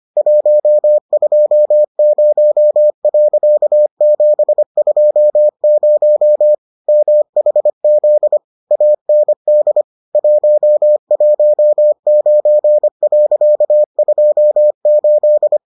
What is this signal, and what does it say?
Morse code, heard as a keyed tone, about 25 wpm, 600 Hz. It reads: 120.720 MHZ AND 119.28